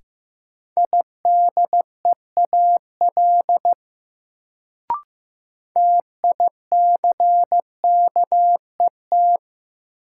Morse code, heard as a keyed tone, about 15 wpm, 700 Hz. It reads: IDEAL TICKET
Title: Sentences from Top 1400 Words - New Word Drill Alt - 15wpm